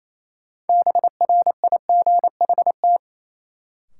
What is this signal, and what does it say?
Morse code, heard as a keyed tone, about 28 wpm, 700 Hz. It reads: BRIGHT
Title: Sentences from Top 1000 Words - New Word Drill - 28wpm